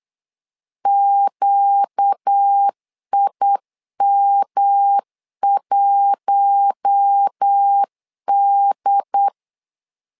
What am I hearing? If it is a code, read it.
Morse code, heard as a keyed tone, about 8 wpm, 800 Hz. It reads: QIM1D
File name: LEZIONI / 4 Q1 8wpm 3035